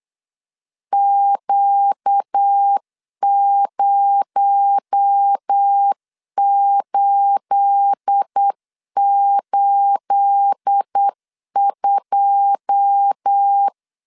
Morse code, heard as a keyed tone, about 8 wpm, 800 Hz. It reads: Q0882
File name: LEZIONI / 9 28 8wpm 3512